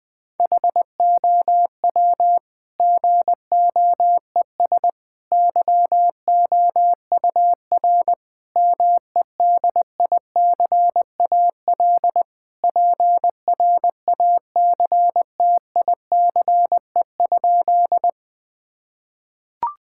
Morse code, heard as a keyed tone, about 20 wpm, 700 Hz. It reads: HOW GOES YOUR MEDICAL PRACTICE?
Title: Sentences from Top 1000 Words - Review Part 5 - 20wpm